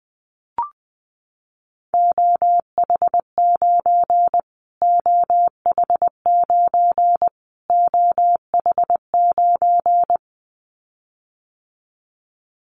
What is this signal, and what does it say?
Morse code, heard as a keyed tone, about 20 wpm, 700 Hz. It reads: OH9 OH9 OH9
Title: Callsign Prefixes for Top 25 DXCC - Repeated 3x - 20wpm